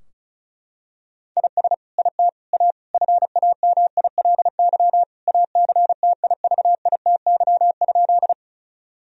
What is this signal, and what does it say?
Morse code, heard as a keyed tone, about 36 wpm, 700 Hz. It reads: IS IT A FAMILY ACTIVITY?
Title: Sentences from Top 1000 Words - Review Part 4 - 35wpm